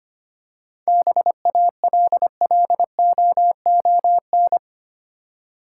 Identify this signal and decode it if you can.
Morse code, heard as a keyed tone, about 25 wpm, 700 Hz. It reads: BALLOON
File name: practice / 25wpm / Easy Words with Hints - 25wpm